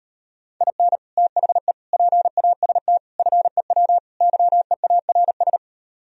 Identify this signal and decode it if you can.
Morse code, heard as a keyed tone, about 38 wpm, 700 Hz. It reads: IN THE PAST FEW YEARS